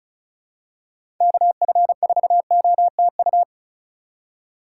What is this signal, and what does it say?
Morse code, heard as a keyed tone, about 36 wpm, 700 Hz. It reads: KF4OTU
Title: USA Call Signs - 6 Characters - 3x Character Spacing - 35wpm